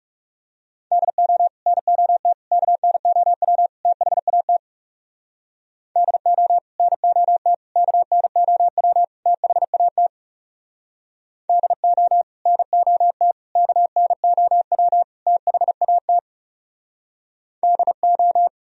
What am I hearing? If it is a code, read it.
Morse code, heard as a keyed tone, about 40 wpm, 700 Hz. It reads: DO NOT KNOW THAT DO NOT KNOW THAT DO NOT KNOW THAT DO